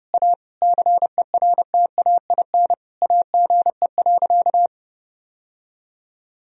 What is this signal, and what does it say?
Morse code, heard as a keyed tone, about 30 wpm, 700 Hz. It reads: A CERTAIN AGE.
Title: Sentences from Top 1000 Words - Review Part 3 - 30wpm